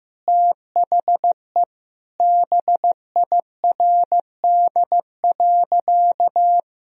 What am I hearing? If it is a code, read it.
Morse code, heard as a keyed tone, about 15 wpm, 700 Hz. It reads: THE BIRD.
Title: Sentences from Top 1000 Words - Review Part 2 - 15wpm